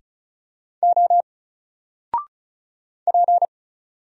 Morse code, heard as a keyed tone, about 35 wpm, 700 Hz. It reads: O P
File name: Single-Letter-Number-35wpm